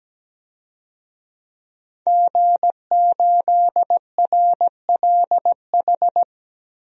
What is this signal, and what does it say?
Morse code, heard as a keyed tone, about 17 wpm, 700 Hz. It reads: G8RLH
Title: DXCC-100 Call Signs - 5 Characters - 17wpm